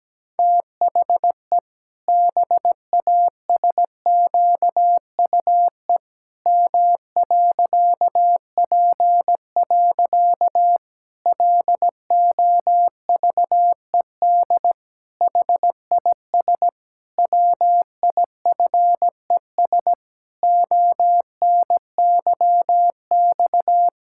Morse code, heard as a keyed tone, about 17 wpm, 700 Hz. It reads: THE BASQUE M.P. LOVED HIS WIFES ONYX
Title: Pangram Challenge v2 - 17wpm